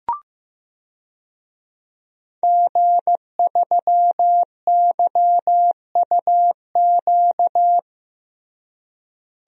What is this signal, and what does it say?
Morse code, heard as a keyed tone, about 15 wpm, 700 Hz. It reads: G3YUQ
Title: DXCC-100 Call Signs - 5 Characters - Repeated 3x - 15wpm